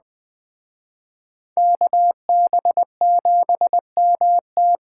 Morse code, heard as a keyed tone, about 20 wpm, 700 Hz. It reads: KB7MT
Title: Speed Racing - USA Call Signs 17wpm